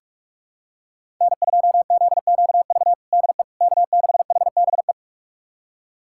Morse code, heard as a keyed tone, about 46 wpm, 700 Hz. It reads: N2ZQV DE K6HBE
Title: USA Callsign DE Callsign 45wpm